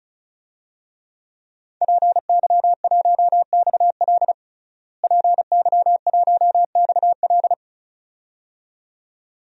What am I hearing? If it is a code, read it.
Morse code, heard as a keyed tone, about 34 wpm, 700 Hz. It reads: PY1XL PY1XL
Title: DXCC-100 Call Signs - 5 Characters - Repeated 2x - 35wpm